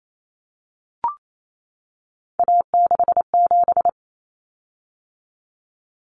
Morse code, heard as a keyed tone, about 28 wpm, 700 Hz. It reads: A67
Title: Common Cut Numbers - 3-Digit 28wpm